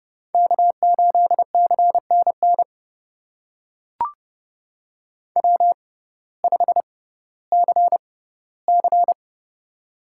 Morse code, heard as a keyed tone, about 30 wpm, spaced as for 16 wpm, 700 Hz. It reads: K8CNN W5CC
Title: USA Call Signs - 6x Character Spacing - 30wpm